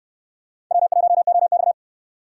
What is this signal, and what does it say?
Morse code, heard as a keyed tone, about 66 wpm, 700 Hz. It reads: W1OX